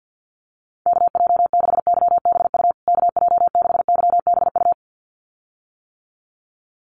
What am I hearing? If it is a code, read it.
Morse code, heard as a keyed tone, about 50 wpm, 700 Hz. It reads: KJ6YBU KJ6YBU
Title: USA Call Signs Repeated 2x 50wpm